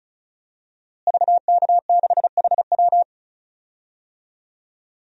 Morse code, heard as a keyed tone, about 34 wpm, 700 Hz. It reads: VK6HW
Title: DXCC-100 Call Signs - 5 Characters - Repeated 2x - 35wpm